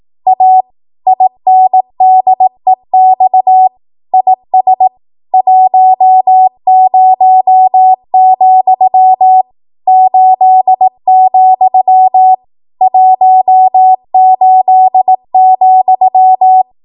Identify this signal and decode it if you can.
Morse code, heard as a keyed tone, about 18 wpm, 750 Hz. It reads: A INDEX IS 10, 8, 18,
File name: Morse / 18WPM